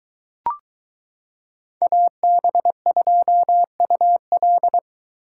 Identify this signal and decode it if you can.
Morse code, heard as a keyed tone, about 22 wpm, 700 Hz. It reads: AB2UL